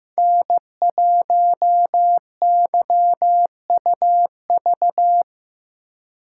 Morse code, heard as a keyed tone, about 15 wpm, 700 Hz. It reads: N1YUV
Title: USA Call Signs - 5 Characters - 6x Character Spacing - 15wpm